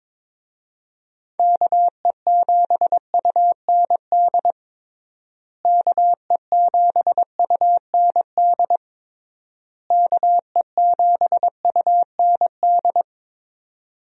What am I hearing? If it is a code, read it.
Morse code, heard as a keyed tone, about 22 wpm, 700 Hz. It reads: KE7UND KE7UND KE7UND
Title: USA Call Signs - 6 Characters - Repeated 3x - 22wpm